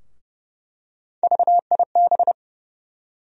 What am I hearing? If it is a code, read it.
Morse code, heard as a keyed tone, about 30 wpm, 700 Hz. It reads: VIB
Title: USA Call Sign Postfixes - 3 Letters 30wpm